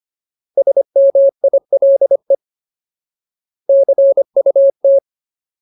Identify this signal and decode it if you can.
Morse code, heard as a keyed tone, about 25 wpm, 550 Hz. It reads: SMILE CUT
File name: Ultimate Word List / 25WPM Ultimate